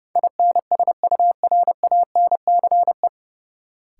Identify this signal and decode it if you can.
Morse code, heard as a keyed tone, about 29 wpm, 700 Hz. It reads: INSURANCE
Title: Sentences from Top 1100 Words - New Word Drill Alt - 30wpm